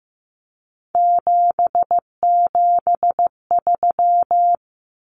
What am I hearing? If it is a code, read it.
Morse code, heard as a keyed tone, about 15 wpm, 700 Hz. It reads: 773
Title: Common Cut Numbers - 3-Digit 15wpm